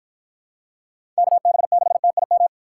Morse code, heard as a keyed tone, about 50 wpm, 700 Hz. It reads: KB6TIM